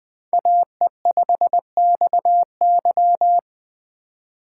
Morse code, heard as a keyed tone, about 21 wpm, 700 Hz. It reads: AE5XY
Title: USA Call Signs - 5 Characters - 4x Character Spacing - 20wpm